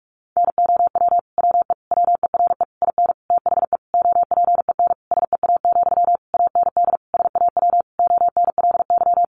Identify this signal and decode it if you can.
Morse code, heard as a keyed tone, about 44 wpm, 700 Hz. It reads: NOW WE WERE IN THE OPEN SEA, AND SAW ONLY
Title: Sentences from Top 1200 Words 45wpm